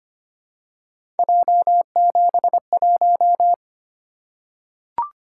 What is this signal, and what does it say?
Morse code, heard as a keyed tone, about 25 wpm, 700 Hz. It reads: J71
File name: Three Letter-Number Combinations - 25wpm